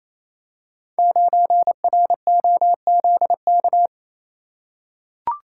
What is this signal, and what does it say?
Morse code, heard as a keyed tone, about 27 wpm, 700 Hz. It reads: 9ROZK
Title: Code Groups - 5 Letter:Numbers - 28wpm